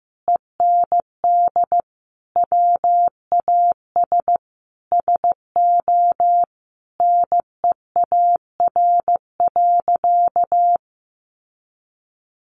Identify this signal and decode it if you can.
Morse code, heard as a keyed tone, about 15 wpm, 700 Hz. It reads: END WAS SO NEAR.
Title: Sentences from Top 300 Words 15wpm